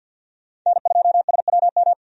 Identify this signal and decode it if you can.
Morse code, heard as a keyed tone, about 53 wpm, 700 Hz. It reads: N2SWM
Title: USA Call Signs - 5 Characters - 50wpm